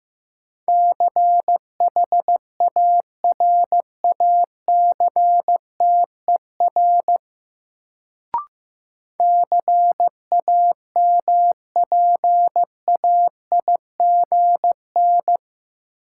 Morse code, heard as a keyed tone, about 15 wpm, 700 Hz. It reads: CHARACTER CAMPAIGN
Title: Sentences from Top 1200 Words - New Word Drill Alt - 15wpm